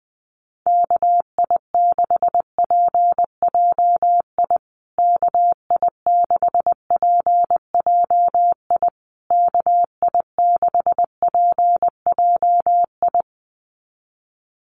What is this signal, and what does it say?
Morse code, heard as a keyed tone, about 20 wpm, 700 Hz. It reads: KI6PJI KI6PJI KI6PJI
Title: USA Call Signs Repeated 3x 20wpm